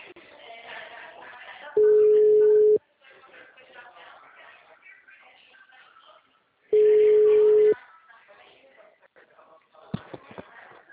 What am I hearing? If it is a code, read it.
Ringback tone (Europe, 425 Hz), with speech.
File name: out- -2003- - - .28158